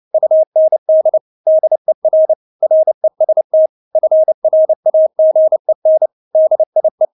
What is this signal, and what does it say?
Morse code, heard as a keyed tone, about 29 wpm, 630 Hz. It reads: UND DER REST FRAGEN DIE
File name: Witz10 29wpm